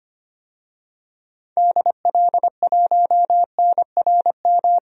Morse code, heard as a keyed tone, about 25 wpm, 700 Hz. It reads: DL1NRM
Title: DXCC-100 Call Signs - 6 Characters - Repeated 2x - 25wpm